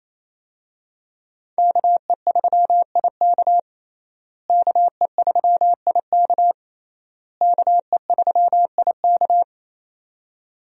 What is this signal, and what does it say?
Morse code, heard as a keyed tone, about 28 wpm, 700 Hz. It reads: KE3IK KE3IK KE3IK
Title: USA Call Signs - 5 Characters - Repeated 3x - 28wpm